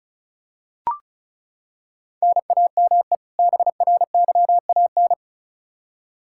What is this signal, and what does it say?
Morse code, heard as a keyed tone, about 36 wpm, 700 Hz. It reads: NAME BRYAN
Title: K1USN SST QSO Practice 35wpm